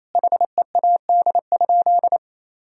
Morse code, heard as a keyed tone, about 28 wpm, 700 Hz. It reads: HEAD?
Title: Sentences from Top 1000 Words - Review Part 1 - 28wpm